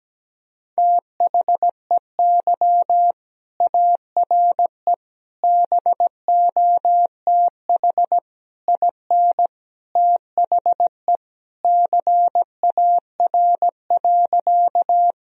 Morse code, heard as a keyed tone, about 17 wpm, 700 Hz. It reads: THEY ARE BOTH IN THE CAR.
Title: Sentences from Top 1000 Words - Review Part 1 - 17wpm